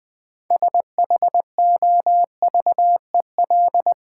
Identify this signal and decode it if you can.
Morse code, heard as a keyed tone, about 20 wpm, 700 Hz. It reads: SHOVEL
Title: Easy Words with Hints - 20wpm